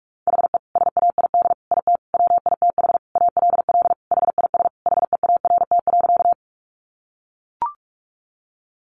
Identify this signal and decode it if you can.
Morse code, heard as a keyed tone, about 44 wpm, 700 Hz. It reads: HE SAID IT WITH ALL HIS HEART.
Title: Sentences from Top 1200 Words 45wpm